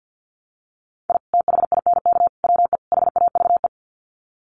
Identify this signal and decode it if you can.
Morse code, heard as a keyed tone, about 50 wpm, 700 Hz. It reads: I THINK WE HAVE
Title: Sets of 4 Words 50wpm